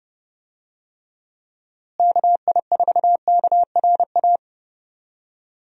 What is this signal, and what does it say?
Morse code, heard as a keyed tone, about 30 wpm, 700 Hz. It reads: KI4KRA
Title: USA Call Signs - 6 Characters - Repeated 3x - 30wpm